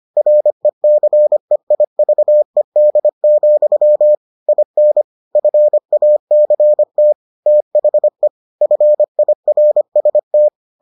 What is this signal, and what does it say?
Morse code, heard as a keyed tone, about 25 wpm, 600 Hz. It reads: RECEIVED, IN FACT THE FIRST